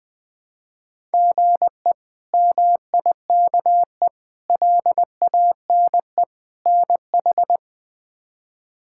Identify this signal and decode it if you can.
Morse code, heard as a keyed tone, about 21 wpm, 700 Hz. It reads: GE MIKE LANE NH
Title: K1USN SST QSO Practice - 20wpm